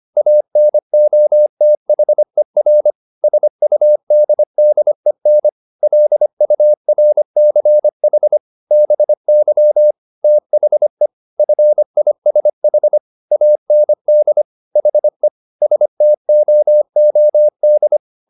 Morse code, heard as a keyed tone, about 25 wpm, 600 Hz. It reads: ANOTHER SUDDEN LURCH BY THE FISH AND HE STOOD
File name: the old man and the sea part 1 25wpm